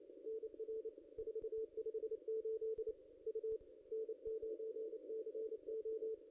Morse code, heard as a keyed tone, about 32 wpm, 440 Hz. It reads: NR 458 U N9CO
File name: contests / ss-cw-k5tr-op-n5rz -0400